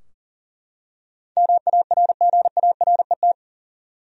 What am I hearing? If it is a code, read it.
Morse code, heard as a keyed tone, about 40 wpm, 700 Hz. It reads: MARGARET
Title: USA Common Names - 40wpm